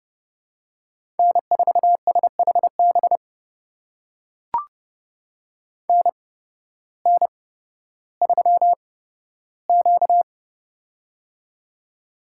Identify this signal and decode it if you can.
Morse code, heard as a keyed tone, about 30 wpm, spaced as for 14 wpm, 700 Hz. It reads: N4SHB NN3Q
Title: USA Call Signs - 8x Character Spacing - 30wpm